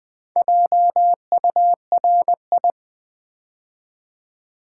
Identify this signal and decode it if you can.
Morse code, heard as a keyed tone, about 20 wpm, 700 Hz. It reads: JURI
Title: Build A Word - 4 Letters At A Time - 12-Letter Words - 20wpm